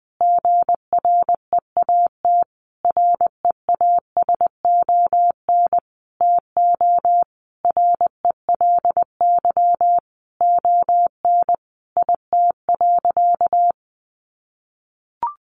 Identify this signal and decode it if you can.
Morse code, heard as a keyed tone, about 20 wpm, 700 Hz. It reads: GREAT REASON TO RELY ON IT.
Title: Sentences from Top 1200 Words 20wpm